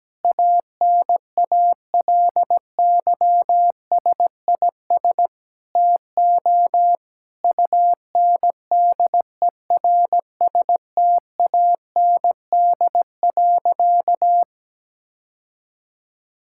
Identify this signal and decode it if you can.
Morse code, heard as a keyed tone, about 17 wpm, 700 Hz. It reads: ANALYSIS TO UNDERSTAND.
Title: Sentences from Top 1000 Words - Review Part 5 - 17wpm